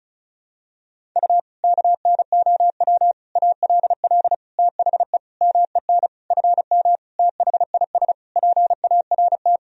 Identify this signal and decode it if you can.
Morse code, heard as a keyed tone, about 36 wpm, 700 Hz. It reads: U KNOW ALL THE MEN FM THIS PART
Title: Sentences from Top 300 Words and Abbreviations 35wpm